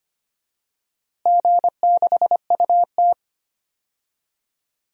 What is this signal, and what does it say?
Morse code, heard as a keyed tone, about 25 wpm, 700 Hz. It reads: G6UT
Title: DXCC-100 Call Signs - 4 Characters - Repeated 2x - 25wpm